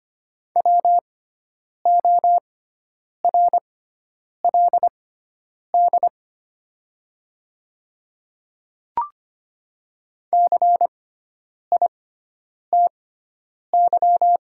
Morse code, heard as a keyed tone, about 25 wpm, spaced as for 9 wpm, 700 Hz. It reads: WORLD CITY
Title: 6x Character Spacing - Top 300 Words - 25wpm